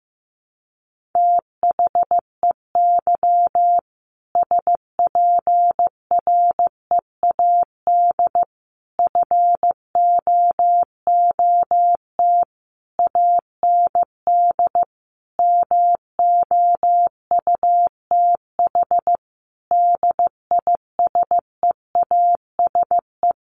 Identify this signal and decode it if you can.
Morse code, heard as a keyed tone, about 15 wpm, 700 Hz. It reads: THEY SPREAD FOOT AND MOUTH DISEASE
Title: Sentences from Top 1500 Words 15wpm